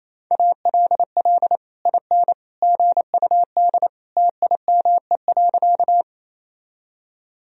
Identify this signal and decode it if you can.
Morse code, heard as a keyed tone, about 28 wpm, 700 Hz. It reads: ALL IN GUD TIME.
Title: Sentences from Top 200 Words and Abbreviations 28wpm